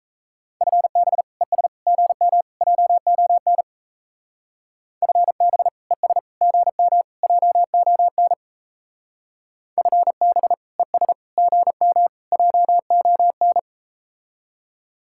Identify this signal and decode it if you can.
Morse code, heard as a keyed tone, about 37 wpm, 700 Hz. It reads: FB ES GM JON FB ES GM JON FB ES GM JON